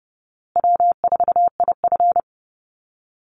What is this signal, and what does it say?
Morse code, heard as a keyed tone, about 30 wpm, 700 Hz. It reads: W4IF
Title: Call Signs Repeated 2x - 30wpm